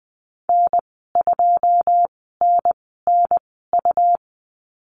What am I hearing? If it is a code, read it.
Morse code, heard as a keyed tone, about 20 wpm, spaced as for 15 wpm, 700 Hz. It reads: N2NNU
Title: 147 Lesson 24g - Letter y - Call Signs 20wpm